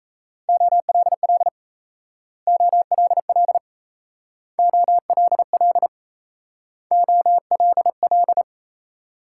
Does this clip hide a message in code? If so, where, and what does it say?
Yes: Morse code from 0.5 to 8.4 s, heard as a keyed tone, about 38 wpm, 700 Hz. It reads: OLL OLL OLL OLL